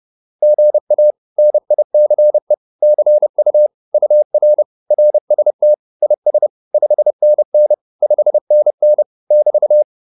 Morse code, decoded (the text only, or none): GA NICE CU UR RST IS 5NN 5NN <BT>